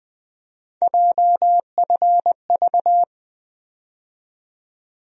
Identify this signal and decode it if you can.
Morse code, heard as a keyed tone, about 21 wpm, 700 Hz. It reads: JFV
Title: USA Call Sign Postfixes - 3 Letters 20wpm